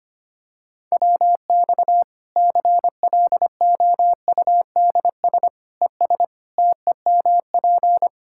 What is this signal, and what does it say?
Morse code, heard as a keyed tone, about 25 wpm, 700 Hz. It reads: WX CLOUDS ES TEMP
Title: Practice QSO - Exchange 2 - Info - 25wpm